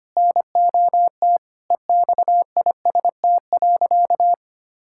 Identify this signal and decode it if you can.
Morse code, heard as a keyed tone, about 25 wpm, 700 Hz. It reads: NOT EXIST.
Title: Sentences from Top 1000 Words - Review Part 5 - 25wpm